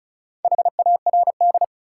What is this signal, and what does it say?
Morse code, heard as a keyed tone, about 36 wpm, 700 Hz. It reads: HARD
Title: Sentences from Top 300 Words - New Word Drill - 35wpm